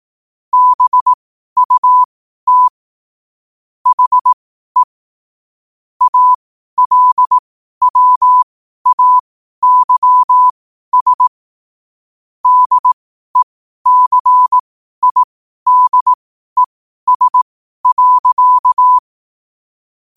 Morse code, heard as a keyed tone, about 18 wpm, spaced as for 12 wpm, 1000 Hz. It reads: BUT HE ALWAYS DECIDES.
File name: qotdfiles / 12wpm